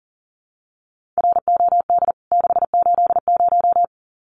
Morse code, heard as a keyed tone, about 40 wpm, 700 Hz. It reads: ROD 680
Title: CWT Exchange 40wpm